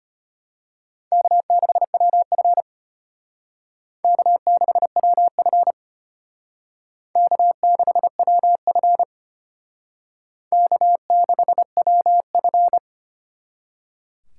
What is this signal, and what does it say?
Morse code, heard as a keyed tone, about 32 wpm, 700 Hz. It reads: K6WF K6WF K6WF K6WF